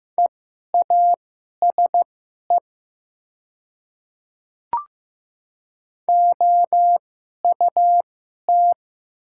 Morse code, heard as a keyed tone, about 15 wpm, spaced as for 11 wpm, 700 Hz. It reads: EASE OUT
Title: 2x Character Spacing - Top 500 Words - 15wpm